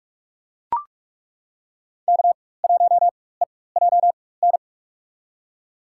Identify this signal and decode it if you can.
Morse code, heard as a keyed tone, about 42 wpm, spaced as for 21 wpm, 700 Hz. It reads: K1EJN
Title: USA Call Signs - 5 Characters - 4x Character Spacing - 45wpm